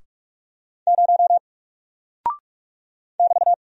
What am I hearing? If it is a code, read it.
Morse code, heard as a keyed tone, about 44 wpm, 700 Hz. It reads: 0 <BT>
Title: Single Character (Letter:Number:Punctuation:Prosign) 45wpm